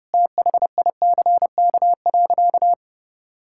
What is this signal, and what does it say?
Morse code, heard as a keyed tone, about 32 wpm, 700 Hz. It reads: THICK.
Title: Sentences from Top 1600 Words 30wpm